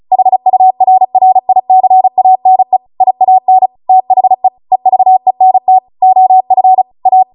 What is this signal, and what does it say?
Morse code, heard as a keyed tone, about 36 wpm, 750 Hz. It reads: HURRICANE IAN THE EVENT OF A
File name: Morse / 35WPM